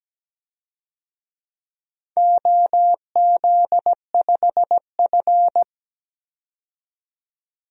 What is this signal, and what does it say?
Morse code, heard as a keyed tone, about 17 wpm, 700 Hz. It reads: OZ5F
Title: DXCC-100 Call Signs - 4 Characters - 17wpm